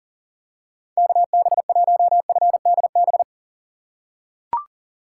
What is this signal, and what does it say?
Morse code, heard as a keyed tone, about 40 wpm, 700 Hz. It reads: KB1FDB
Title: USA Call Signs - 6 Characters - 6x Character Spacing - 40wpm